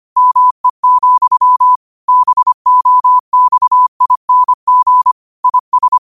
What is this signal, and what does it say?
Morse code, heard as a keyed tone, about 25 wpm, 1000 Hz. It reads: ME, BOXING IS
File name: qotdfiles / 25wpm